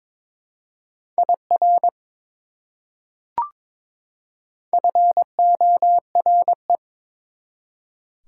Morse code, heard as a keyed tone, about 22 wpm, 700 Hz. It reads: IR FORE
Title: Prefixes and Suffixes 22wpm